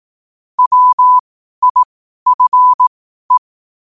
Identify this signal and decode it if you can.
Morse code, heard as a keyed tone, about 18 wpm, spaced as for 12 wpm, 1000 Hz. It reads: WIFE
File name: qotdfiles / 12wpm